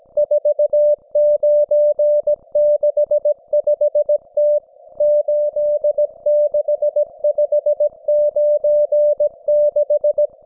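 Morse code, heard as a keyed tone, about 17 wpm, 590 Hz. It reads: 4965T 86596
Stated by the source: Russian military Morse station, M32, RDL, on 5268 kHz.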